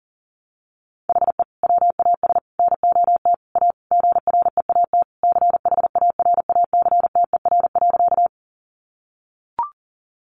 Morse code, heard as a keyed tone, about 40 wpm, 700 Hz. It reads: HE WAS NOT A GREAT CHARACTER.
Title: Sentences from Top 1200 Words 40wpm